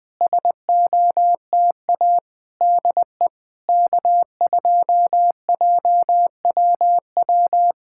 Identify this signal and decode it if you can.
Morse code, heard as a keyed tone, about 20 wpm, 700 Hz. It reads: SOTA DE K2JWW
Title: SOTA QSO Practice 20wpm